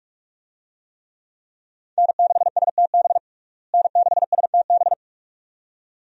Morse code, heard as a keyed tone, about 46 wpm, 700 Hz. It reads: N6STB N6STB
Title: USA Call Signs - 5 Characters - Repeated 2x - 45wpm